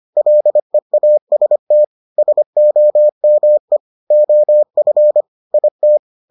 Morse code, heard as a keyed tone, about 25 wpm, 600 Hz. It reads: LEAST SOME OF IT